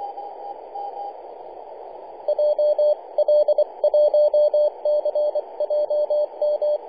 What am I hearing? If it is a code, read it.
Morse code, heard as a keyed tone, about 24 wpm, 600 Hz, with radio noise. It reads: JL1CJM
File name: kl7ky-50-085mhz-cw